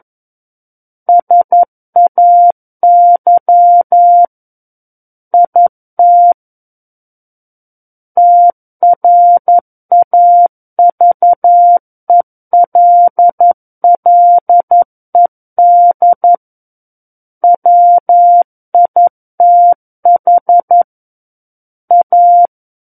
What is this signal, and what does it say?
Morse code, heard as a keyed tone, about 11 wpm, 700 Hz. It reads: SAY IT TRAVELLED WITH A
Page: SKCC - War of the Worlds - 02-Chapter 2 - 11 WPM